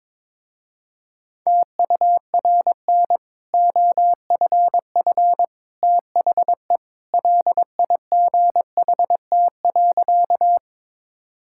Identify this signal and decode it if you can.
Morse code, heard as a keyed tone, about 22 wpm, 700 Hz. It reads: TURN OFF THE LIGHT.
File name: practice / 22wpm / Easy Sentences with Hints - 22wpm